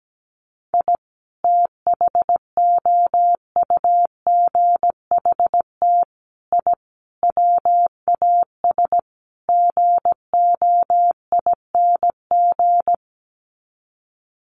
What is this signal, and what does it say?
Morse code, heard as a keyed tone, about 17 wpm, 700 Hz. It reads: I THOUGHT I WAS GOING
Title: Sets of 5 Words Encore2 17wpm